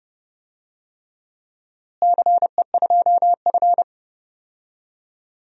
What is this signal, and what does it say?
Morse code, heard as a keyed tone, about 32 wpm, 700 Hz. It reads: CE2F
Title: DXCC-100 Call Signs - 4 Characters - Repeated 3x - 30wpm